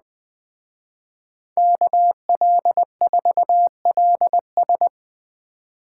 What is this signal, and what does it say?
Morse code, heard as a keyed tone, about 21 wpm, 700 Hz. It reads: KL4LS